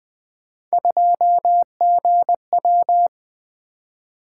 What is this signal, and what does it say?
Morse code, heard as a keyed tone, about 20 wpm, 700 Hz. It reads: 2GW
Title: Three Letter-Number Combinations - 20wpm